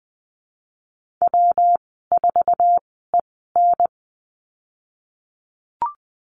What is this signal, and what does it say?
Morse code, heard as a keyed tone, about 20 wpm, spaced as for 15 wpm, 700 Hz. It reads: W4EN
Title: 154 Lesson 25g - Letter p - Call Signs 20wpm